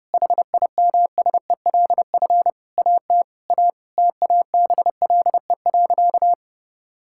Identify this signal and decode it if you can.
Morse code, heard as a keyed tone, about 32 wpm, 700 Hz. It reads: HIMSELF AT A TABLE.
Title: Sentences from Top 1600 Words 30wpm